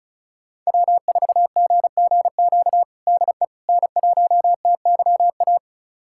Morse code, heard as a keyed tone, about 36 wpm, 700 Hz. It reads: W4GGQ DE N1TYA
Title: USA Callsign DE Callsign - 5 Characters - 35wpm